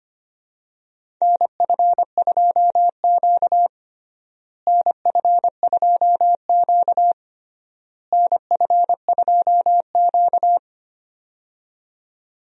Morse code, heard as a keyed tone, about 25 wpm, 700 Hz. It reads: NF2Q NF2Q NF2Q
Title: USA Call Signs - 4 Characters - Repeated 3x - 25wpm